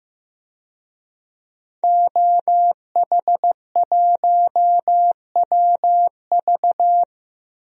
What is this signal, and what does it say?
Morse code, heard as a keyed tone, about 15 wpm, 700 Hz. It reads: OH1WV
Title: DXCC-100 Call Signs - 5 Characters - 15wpm